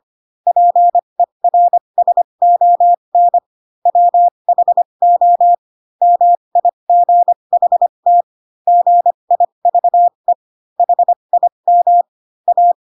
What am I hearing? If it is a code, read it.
Morse code, heard as a keyed tone, about 25 wpm, spaced as for 22 wpm, 700 Hz. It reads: PERSON WHO MIGHT GIVE HIM A